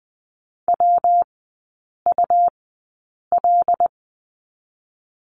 Morse code, heard as a keyed tone, about 20 wpm, spaced as for 8 wpm, 700 Hz. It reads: WUL
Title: 145 Lesson 24e - Letter y - Three Characters 20wpm